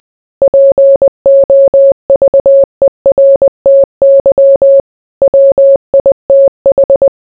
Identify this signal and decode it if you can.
Morse code, heard as a keyed tone, about 20 wpm, 550 Hz. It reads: POVERTY WITH